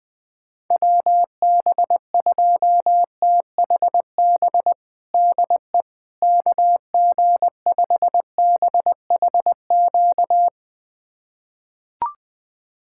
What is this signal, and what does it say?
Morse code, heard as a keyed tone, about 21 wpm, 700 Hz. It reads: WB2THB DE KG5BHQ
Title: USA Callsign DE Callsign 20wpm